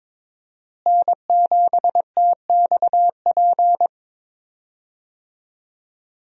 Morse code, heard as a keyed tone, about 22 wpm, 700 Hz. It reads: N7TXP
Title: USA Call Signs - 5 Characters - Repeated 2x - 22wpm